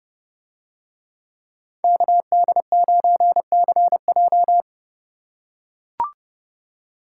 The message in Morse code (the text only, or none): KD9CJ